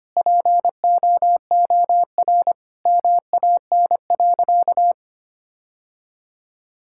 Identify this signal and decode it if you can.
Morse code, heard as a keyed tone, about 25 wpm, 700 Hz. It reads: POOR MAN.
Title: Sentences from Top 1000 Words - Review Part 5 - 25wpm